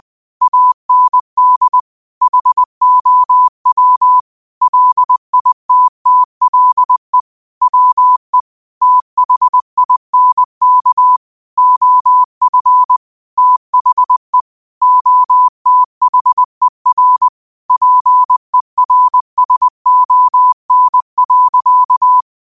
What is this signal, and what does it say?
Morse code, heard as a keyed tone, about 20 wpm, 1000 Hz. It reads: AND HOW LITTLE WE THINK OF THE OTHER PERSON.
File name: 20wpm